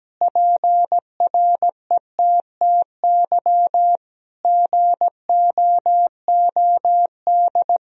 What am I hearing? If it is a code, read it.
Morse code, heard as a keyed tone, about 17 wpm, 700 Hz. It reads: PRETTY GOOD
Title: Sentences from Top 1000 Words - Review Part 3 - 17wpm